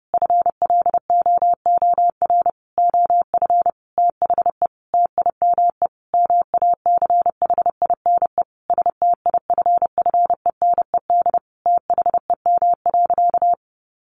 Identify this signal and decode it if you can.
Morse code, heard as a keyed tone, about 30 wpm, 700 Hz. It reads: FLOOR OF THE TIME MACHINE STIFFENED THEM.
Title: A Sound Of Thunder 30wpm